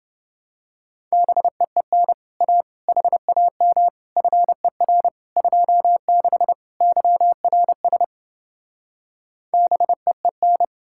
Morse code, heard as a keyed tone, about 28 wpm, 700 Hz. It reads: BEEN A HAM FER 26 YRS BEEN